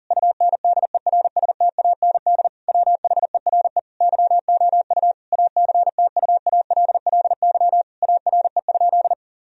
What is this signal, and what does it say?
Morse code, heard as a keyed tone, about 42 wpm, 700 Hz. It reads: UNDERSTAND WHERE YOU ACTUALLY ARE?
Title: Sentences from Top 1000 Words - Review Part 4 - 40wpm